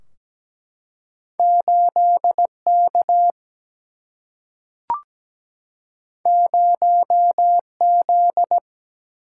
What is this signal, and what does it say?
Morse code, heard as a keyed tone, about 17 wpm, 700 Hz. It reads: 8K 0Z
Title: Single-Letter Single-Number 17wpm